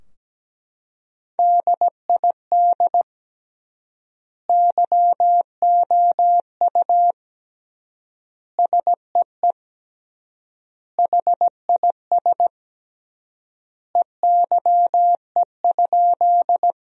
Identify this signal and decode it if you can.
Morse code, heard as a keyed tone, about 17 wpm, 700 Hz. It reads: DID YOU SEE HIS EYE?
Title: 3x Word-Spacing - Sentences from Top 300 Words - 17wpm